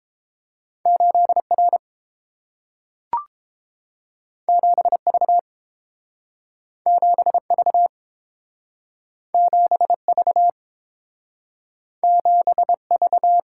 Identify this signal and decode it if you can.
Morse code, heard as a keyed tone, about 30 wpm, 700 Hz. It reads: 8 R 7 V 7 V 7 V 7 V